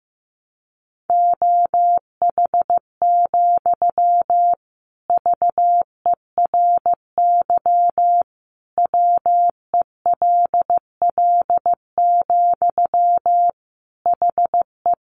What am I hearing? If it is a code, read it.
Morse code, heard as a keyed tone, about 14 wpm, 700 Hz. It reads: OH, VERY WELL, HE
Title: Sentences from Top 500 Words 15wpm